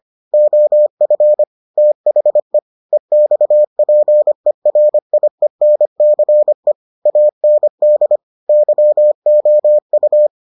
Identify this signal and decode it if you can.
Morse code, heard as a keyed tone, about 25 wpm, 600 Hz. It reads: OF THE EXPERIENCE AND YOU